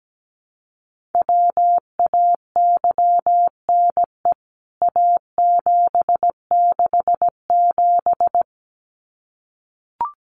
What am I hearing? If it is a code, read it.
Morse code, heard as a keyed tone, about 17 wpm, 700 Hz. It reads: WAYNE A767
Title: CWT Exchange 17wpm